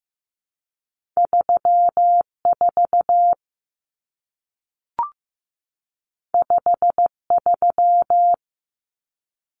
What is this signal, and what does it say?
Morse code, heard as a keyed tone, about 15 wpm, 700 Hz. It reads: 3 4 5 3
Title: Numbers - 2-Digit 15wpm